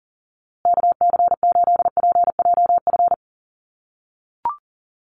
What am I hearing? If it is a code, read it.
Morse code, heard as a keyed tone, about 40 wpm, 700 Hz. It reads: KC8PJF
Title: USA Call Signs Repeated 3x 40wpm